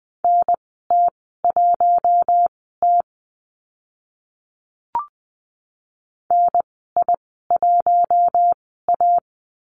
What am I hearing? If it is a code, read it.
Morse code, heard as a keyed tone, about 20 wpm, spaced as for 14 wpm, 700 Hz. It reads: NT1T NI1A
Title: 042 Lesson 8g - Number 1 - Call Signs 20wpm